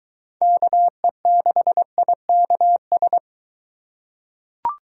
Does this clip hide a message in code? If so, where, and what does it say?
Yes: Morse code from 0.4 to 3.2 s, heard as a keyed tone, about 22 wpm, 700 Hz. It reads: KE6IKS